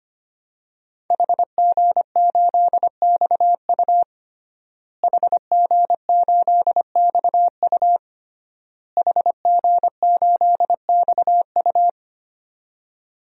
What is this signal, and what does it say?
Morse code, heard as a keyed tone, about 25 wpm, 700 Hz. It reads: HG8XU HG8XU HG8XU
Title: DXCC-100 Call Signs - 5 Characters - Repeated 3x - 25wpm